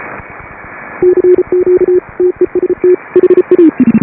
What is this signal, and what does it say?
Morse code, heard as a keyed tone, about 32 wpm, 350 Hz. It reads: CQ TEST HA
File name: AUDIO2018 / HA5BA CW